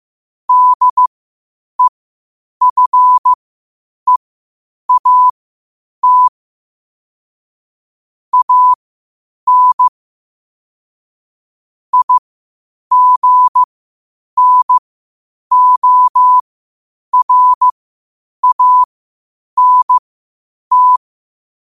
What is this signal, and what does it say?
Morse code, heard as a keyed tone, about 15 wpm, spaced as for 8 wpm, 1000 Hz. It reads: DEFEAT AN IGNORANT
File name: qotdfiles / 8wpm